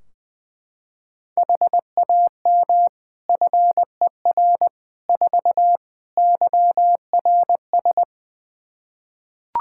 Morse code, heard as a keyed tone, about 20 wpm, 700 Hz. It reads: HAM FER 4 YRS
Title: Practice QSO - Exchange 3 - More Info - 20wpm